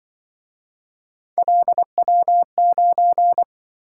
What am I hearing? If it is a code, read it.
Morse code, heard as a keyed tone, about 25 wpm, 700 Hz. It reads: LW9
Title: Speed-Racing - Three Letter-Number Combinations - 20wpm